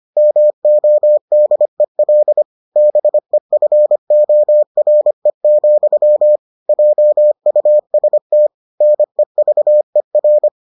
Morse code, heard as a keyed tone, about 25 wpm, 600 Hz. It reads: MODEL BEFORE, JUST NEVER